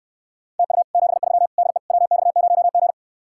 Morse code, heard as a keyed tone, about 69 wpm, 700 Hz. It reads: TU 73 DE WX0G